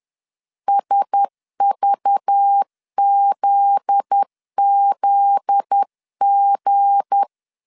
Morse code, heard as a keyed tone, about 10 wpm, 800 Hz. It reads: SVZZG
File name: LEZIONI / 7 GF9Z / 7 GF9Z 10wpm 2611